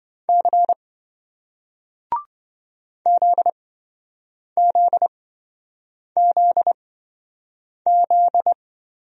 Morse code, heard as a keyed tone, about 26 wpm, spaced as for 7 wpm, 700 Hz. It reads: C ZZZZ